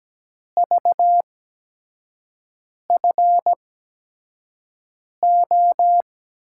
Morse code, heard as a keyed tone, about 17 wpm, spaced as for 5 wpm, 700 Hz. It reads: VFO
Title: USA Call Signs - 6 Characters - 8x Character Spacing - 17wpm